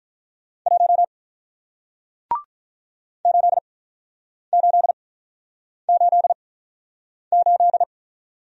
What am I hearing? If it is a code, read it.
Morse code, heard as a keyed tone, about 48 wpm, spaced as for 9 wpm, 700 Hz. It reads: 1 8888